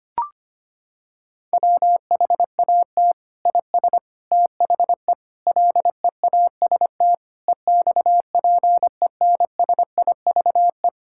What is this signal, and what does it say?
Morse code, heard as a keyed tone, about 25 wpm, 700 Hz. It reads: WHAT IS THE LEAST EXPENSIVE
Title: Sentences from Top 1000 Words - Review Part 6 - 25wpm